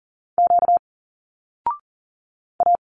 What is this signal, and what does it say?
Morse code, heard as a keyed tone, about 42 wpm, 700 Hz. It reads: Q A
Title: Single Letters 40wpm